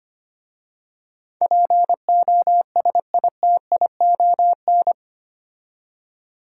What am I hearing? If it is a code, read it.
Morse code, heard as a keyed tone, about 25 wpm, 700 Hz. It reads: POSITION
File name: 8 Letter Words - First 3 Letters Given - 25wpm